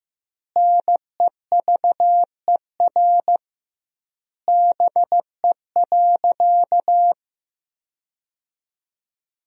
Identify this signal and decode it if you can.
Morse code, heard as a keyed tone, about 15 wpm, 700 Hz. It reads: NEVER BE.
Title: 2x Word-Spacing - Sentences from Top 300 Words - 15wpm